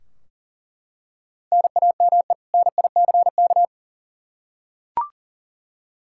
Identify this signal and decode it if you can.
Morse code, heard as a keyed tone, about 37 wpm, 700 Hz. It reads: NAME NICK
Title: Practice QSO - Exchange 1 - Beginning - 40wpm